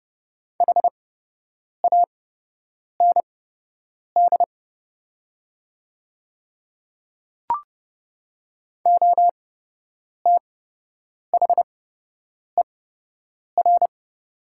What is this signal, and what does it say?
Morse code, heard as a keyed tone, about 30 wpm, spaced as for 8 wpm, 700 Hz. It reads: HAND OTHER
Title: 8x Character Spacing - Top 300 Words - 30wpm